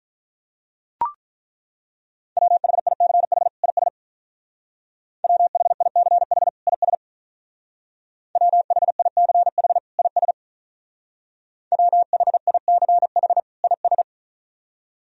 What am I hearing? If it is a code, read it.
Morse code, heard as a keyed tone, about 43 wpm, 700 Hz. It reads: WHICH IS WHICH IS WHICH IS WHICH IS
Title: Speed-Racing - Sets of 2 Words 35wpm